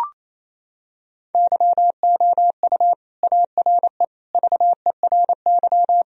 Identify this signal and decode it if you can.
Morse code, heard as a keyed tone, about 28 wpm, 700 Hz. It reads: YOU ARE VERY
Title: Sentences from Top 1000 Words - Review Part 1 - 28wpm